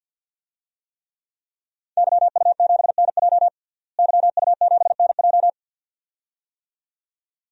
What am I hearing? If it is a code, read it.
Morse code, heard as a keyed tone, about 51 wpm, 700 Hz. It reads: YU7NJ YU7NJ
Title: DXCC-100 Call Signs - 5 Characters - Repeated 2x - 50wpm